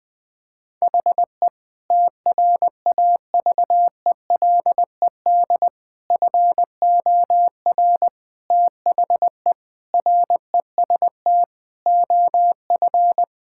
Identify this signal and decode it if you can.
Morse code, heard as a keyed tone, about 21 wpm, 700 Hz. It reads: HE TRAVELED FOR THE REST OF
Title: Sentences with 5-Word Phrases - Spoken Beginning - 20wpm